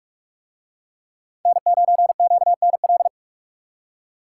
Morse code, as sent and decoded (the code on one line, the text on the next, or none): -. ----. --.- -. .-..
N9QNL